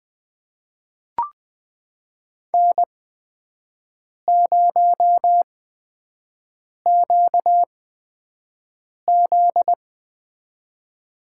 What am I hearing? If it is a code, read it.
Morse code, heard as a keyed tone, about 20 wpm, spaced as for 5 wpm, 700 Hz. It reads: N0QZ
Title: USA Call Signs - 4 Characters - 8x Character Spacing - 20wpm